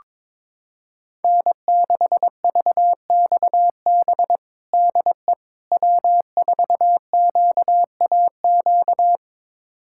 Morse code, heard as a keyed tone, about 22 wpm, 700 Hz. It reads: N6VXB DE W4QAQ
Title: USA Callsign DE Callsign - 5 Characters - 22wpm